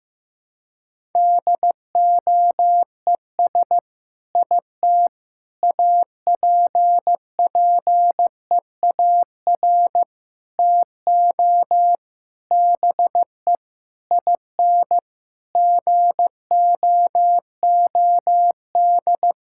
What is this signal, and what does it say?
Morse code, heard as a keyed tone, about 15 wpm, 700 Hz. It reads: DOES IT APPEAR TO BE IN GOOD
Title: Sentences from Top 1000 Words - Review Part 6 - 15wpm